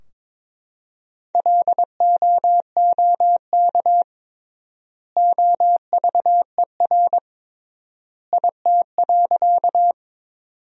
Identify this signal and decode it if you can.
Morse code, heard as a keyed tone, about 22 wpm, 700 Hz. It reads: LOOK OVER IT.
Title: 3x Word-Spacing - Sentences from Top 100 Words - 22wpm